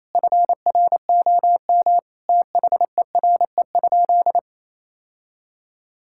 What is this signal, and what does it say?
Morse code, heard as a keyed tone, about 28 wpm, 700 Hz. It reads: FROM THERE?
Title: Sentences from Top 1000 Words - Review Part 5 - 28wpm